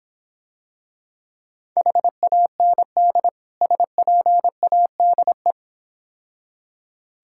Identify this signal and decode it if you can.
Morse code, heard as a keyed tone, about 26 wpm, 700 Hz. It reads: HAND SPADE